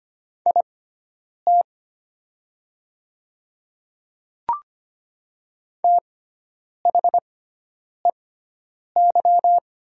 Morse code, heard as a keyed tone, about 25 wpm, spaced as for 9 wpm, 700 Hz. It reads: IT THEY
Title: 6x Character Spacing - Top 200 Words - 25wpm